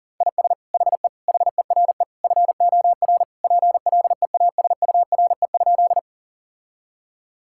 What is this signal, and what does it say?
Morse code, heard as a keyed tone, about 40 wpm, 700 Hz. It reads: IS HE HERE FOR PLEASURE?
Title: Sentences from Top 1600 Words 40wpm